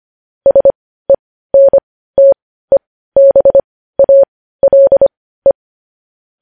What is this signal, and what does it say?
Morse code, heard as a keyed tone, about 25 wpm, spaced as for 15 wpm, 550 Hz. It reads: SENTEBALE
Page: News Headlines in Morse Code at 15 WPM